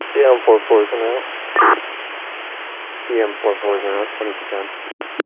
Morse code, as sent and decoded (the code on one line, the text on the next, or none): none